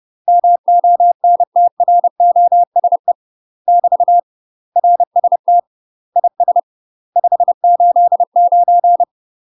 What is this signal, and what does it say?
Morse code, heard as a keyed tone, about 30 wpm, 700 Hz. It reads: MONTROSE <BT> RST IS 589
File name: UK MASTER QSOs TRAINING / 30WPM / UK MASTER QSOs 0007